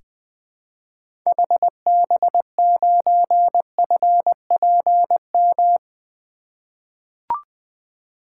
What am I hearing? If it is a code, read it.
Morse code, heard as a keyed tone, about 21 wpm, 700 Hz. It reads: HB9FPM
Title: 2023 SOTA Activator Call Signs - 20wpm